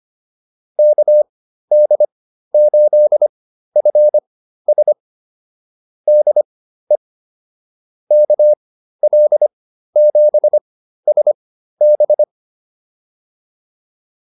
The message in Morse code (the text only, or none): KD8FS DE KL7SB